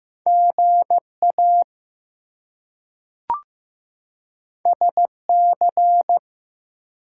Morse code, heard as a keyed tone, about 15 wpm, 700 Hz. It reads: GA SC
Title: US States and Canadian Provinces 15wpm